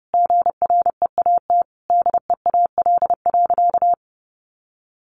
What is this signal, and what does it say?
Morse code, heard as a keyed tone, about 30 wpm, 700 Hz. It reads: GREAT DEAL.
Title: Sentences from Top 1500 Words 30wpm